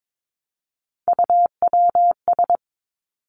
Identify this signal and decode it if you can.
Morse code, heard as a keyed tone, about 22 wpm, 700 Hz. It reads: UWS
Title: Three Challenging Letter-Number Combinations 22wpm